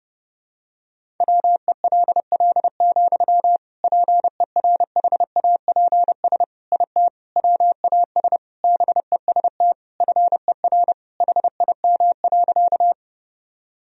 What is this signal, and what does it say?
Morse code, heard as a keyed tone, about 30 wpm, 700 Hz. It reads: WELL, PERHAPS IT WAS BEST FER HIM.
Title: Sentences from Top 500 Words and Abbreviations 30wpm